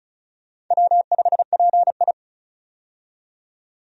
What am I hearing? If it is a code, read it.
Morse code, heard as a keyed tone, about 36 wpm, 700 Hz. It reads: W5PI
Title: USA Call Signs - 4 Characters - 35wpm